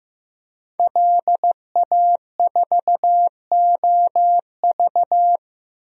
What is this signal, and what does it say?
Morse code, heard as a keyed tone, about 16 wpm, 700 Hz. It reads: LA4OV
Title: DXCC-100 Call Signs - 5 Characters - Repeated 3x - 15wpm